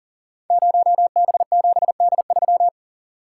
Morse code, heard as a keyed tone, about 41 wpm, 690 Hz. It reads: 0B7D3
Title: Code Groups - 5 Letter:Numbers - 40wpm